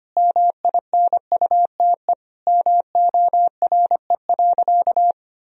Morse code, heard as a keyed tone, about 25 wpm, 700 Hz. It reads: MINUTE MORE.
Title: Sentences from Top 1000 Words - Review Part 4 - 25wpm